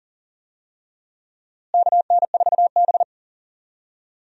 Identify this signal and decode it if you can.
Morse code, heard as a keyed tone, about 40 wpm, 700 Hz. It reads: KN4B
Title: USA Call Signs - 4 Characters - Repeated 3x - 40wpm